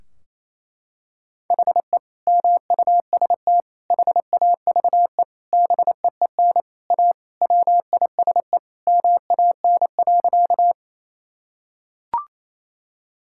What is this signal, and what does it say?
Morse code, heard as a keyed tone, about 28 wpm, 700 Hz. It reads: HE MUST HAVE BEEN A WISE MAN.
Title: Sentences from Top 1600 Words 28wpm